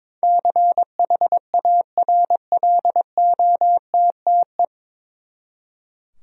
Morse code, heard as a keyed tone, about 22 wpm, 700 Hz. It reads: CHARLOTTE
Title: USA Common Names - 22wpm